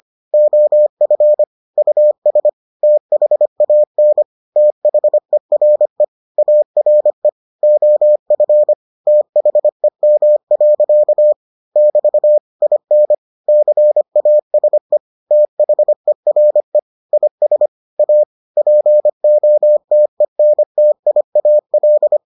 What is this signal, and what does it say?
Morse code, heard as a keyed tone, about 25 wpm, 600 Hz. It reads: OF US THAN THERE ARE OF THEM. <BT> IN CASE THERE IS A POTENTIAL